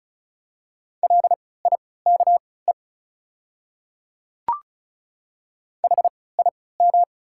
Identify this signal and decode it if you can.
Morse code, heard as a keyed tone, about 33 wpm, spaced as for 19 wpm, 690 Hz. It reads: LIKE HIM
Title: 3x Character Spacing - Top 100 Words - 35wpm